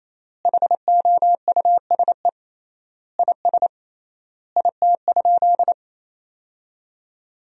Morse code, heard as a keyed tone, about 28 wpm, 700 Hz. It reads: HOUSE IS IT?
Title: 3x Word-Spacing - Sentences from Top 200 Words - 28wpm